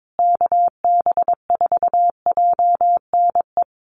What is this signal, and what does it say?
Morse code, heard as a keyed tone, about 22 wpm, 700 Hz. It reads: KB4JNE
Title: USA Call Signs Repeated 3x 22wpm